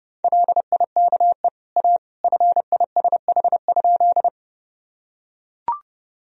Morse code, heard as a keyed tone, about 30 wpm, 700 Hz. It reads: LIKE A FISH?
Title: Sentences from Top 1000 Words - Review Part 1 - 30wpm